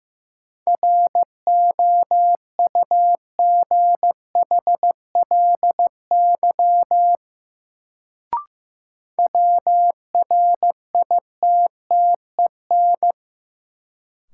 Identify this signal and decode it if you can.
Morse code, heard as a keyed tone, about 15 wpm, 700 Hz. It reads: ROUGHLY WRITTEN
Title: Sentences from Top 1500 Words - New Word Drill - 15wpm